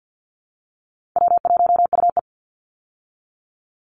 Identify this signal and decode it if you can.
Morse code, heard as a keyed tone, about 50 wpm, 700 Hz. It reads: W1UE
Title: Call Signs Repeated 3x - 50wpm